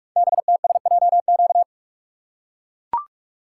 Morse code, heard as a keyed tone, about 44 wpm, 700 Hz. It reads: BTSJQ
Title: Code Groups - 5 Letters - 45wpm